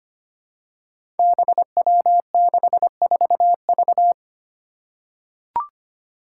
Morse code, heard as a keyed tone, about 25 wpm, 700 Hz. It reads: BW64V
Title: Five Challenging Letter-Number Combinations 25wpm